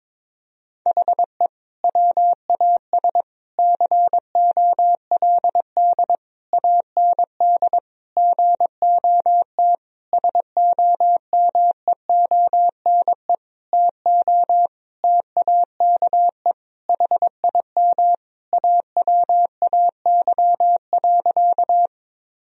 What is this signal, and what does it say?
Morse code, heard as a keyed tone, about 22 wpm, 700 Hz. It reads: HE WAS COLD AND GOT SOMEONE TO TAKE HIM AWAY.
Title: Sentences from Top 1000 Words - Review Part 4 - 22wpm